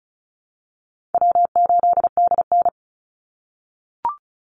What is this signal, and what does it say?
Morse code, heard as a keyed tone, about 34 wpm, 700 Hz. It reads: W8DN
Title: Call Signs - 35wpm